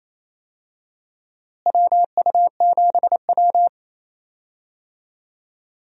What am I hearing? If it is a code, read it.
Morse code, heard as a keyed tone, about 28 wpm, 700 Hz. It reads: WU7W
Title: USA Call Signs - 4 Characters - Repeated 2x - 28wpm